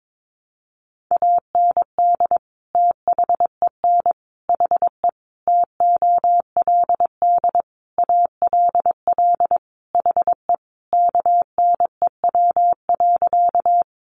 Morse code, heard as a keyed tone, about 22 wpm, 700 Hz. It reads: AND THEN HE TOLD ALL HE KNEW.
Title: Sentences from Top 400 Words 22wpm